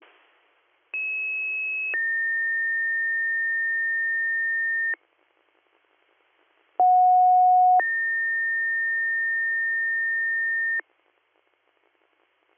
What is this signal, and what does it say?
Two-tone page, heard as a one-second tone, then a three-second tone, about 2600 then 1800 Hz, about 730 then 1800 Hz. It is heard from 0.9 to 10.8 s.